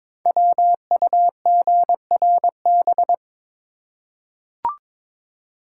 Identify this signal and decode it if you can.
Morse code, heard as a keyed tone, about 23 wpm, 700 Hz. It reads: WUGRB
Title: Code Groups - 5 Letters - 22wpm